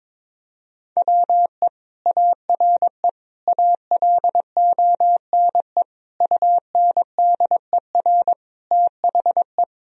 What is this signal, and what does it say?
Morse code, heard as a keyed tone, about 21 wpm, 700 Hz. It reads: WE ARE ALONE UNDER THE
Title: Sentences from Top 1000 Words - Review Part 1 - 22wpm